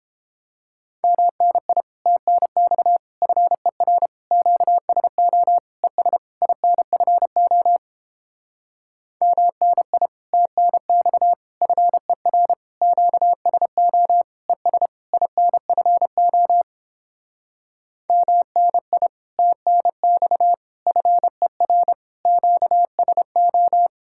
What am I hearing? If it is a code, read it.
Morse code, heard as a keyed tone, about 31 wpm, 700 Hz. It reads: MNI TNX FER QSO ES INFO MNI TNX FER QSO ES INFO MNI TNX FER QSO